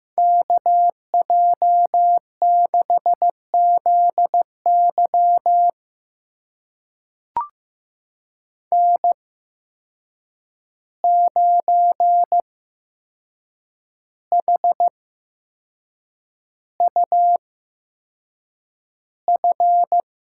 Morse code, heard as a keyed tone, about 15 wpm, spaced as for 6 wpm, 700 Hz. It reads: KJ6ZY N9HUF
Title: USA Call Signs - 5 Characters - 8x Character Spacing - 15wpm